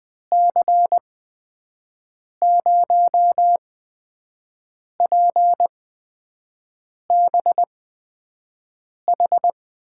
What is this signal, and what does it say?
Morse code, heard as a keyed tone, about 20 wpm, spaced as for 5 wpm, 700 Hz. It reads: C0PBH
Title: USA Call Signs - 6 Characters - 8x Character Spacing - 20wpm